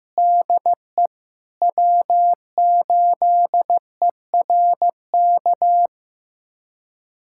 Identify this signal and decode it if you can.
Morse code, heard as a keyed tone, about 15 wpm, 700 Hz. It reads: DE W8ERK
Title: USA Callsign DE Callsign - 5 Characters - 15wpm